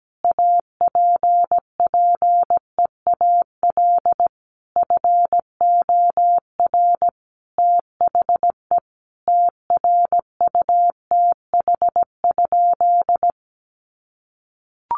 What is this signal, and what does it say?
Morse code, heard as a keyed tone, about 17 wpm, 700 Hz. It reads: APPEAL FOR THE TRUTH?
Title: Sentences from Top 1300 Words 17wpm